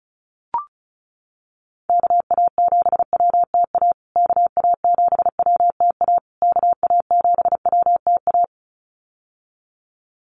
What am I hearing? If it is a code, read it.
Morse code, heard as a keyed tone, about 34 wpm, 700 Hz. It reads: KA7WTA KA7WTA KA7WTA
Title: USA Call Signs Repeated 3x 35wpm